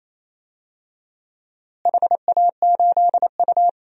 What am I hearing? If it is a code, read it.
Morse code, heard as a keyed tone, about 28 wpm, 700 Hz. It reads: HA8U
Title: DXCC-100 Call Signs - 4 Characters - Repeated 2x - 28wpm